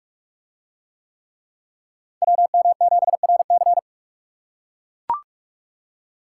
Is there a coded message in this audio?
Yes: Morse code from 2.2 to 3.8 s, heard as a keyed tone, about 42 wpm, 700 Hz. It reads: WM7RC